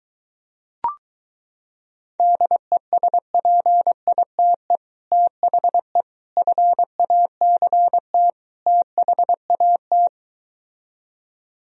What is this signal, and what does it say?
Morse code, heard as a keyed tone, about 23 wpm, 700 Hz. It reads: DESPITE THE FACT THAT